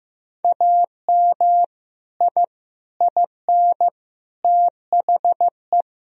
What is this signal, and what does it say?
Morse code, heard as a keyed tone, about 15 wpm, 700 Hz. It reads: AM I IN THE
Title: Sentences from Top 1000 Words - Review Part 2 - 15wpm